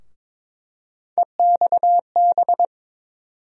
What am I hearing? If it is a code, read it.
Morse code, heard as a keyed tone, about 22 wpm, 700 Hz. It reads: EXB
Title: USA Call Sign Postfixes - 3 Letters 22wpm